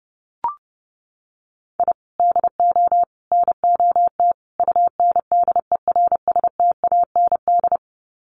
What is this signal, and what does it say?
Morse code, heard as a keyed tone, about 28 wpm, 700 Hz. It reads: I DO NOT UNDERSTAND
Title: Sets of 4 Words 30wpm